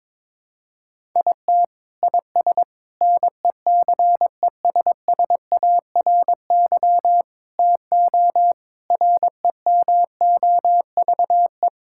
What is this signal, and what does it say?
Morse code, heard as a keyed tone, about 22 wpm, 700 Hz. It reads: IT IS NECESSARY TO REMOVE
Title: Sentences from Top 1000 Words - Review Part 5 - 22wpm